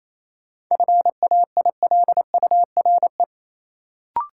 Morse code, heard as a keyed tone, about 28 wpm, 700 Hz. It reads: FAILURE
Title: Sentences from Top 900 Words - New Word Drill - 28wpm